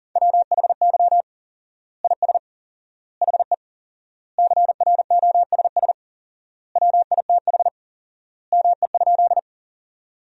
Morse code, heard as a keyed tone, about 40 wpm, 700 Hz. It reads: WHY IS HE CROSS WITH ME?
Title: 4x Word-Spacing - Sentences from Top 300 Words - 40wpm